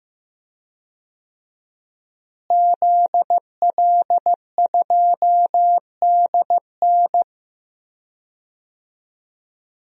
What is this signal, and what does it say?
Morse code, heard as a keyed tone, about 15 wpm, 700 Hz. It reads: ZL2DN
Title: DXCC-100 Call Signs - 5 Characters - Repeated 2x - 15wpm